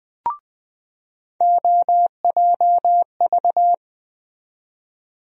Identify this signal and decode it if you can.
Morse code, heard as a keyed tone, about 20 wpm, 700 Hz. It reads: OJV
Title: USA Call Sign Postfixes - 3 Letters 20wpm